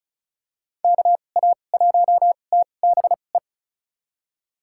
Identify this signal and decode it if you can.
Morse code, heard as a keyed tone, about 36 wpm, spaced as for 25 wpm, 700 Hz. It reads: KA1TBE
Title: USA Call Signs - 6 Characters - 2x Character Spacing - 35wpm